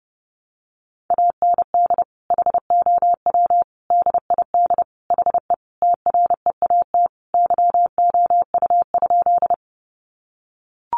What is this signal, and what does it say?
Morse code, heard as a keyed tone, about 30 wpm, 700 Hz. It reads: AND HOW DID HE TREAT YOU?
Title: Sentences from Top 1300 Words 30wpm